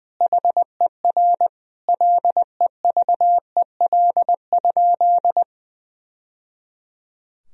Morse code, heard as a keyed tone, about 21 wpm, 700 Hz. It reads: HER LEVEL?
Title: Sentences from Top 1000 Words - Review Part 4 - 20wpm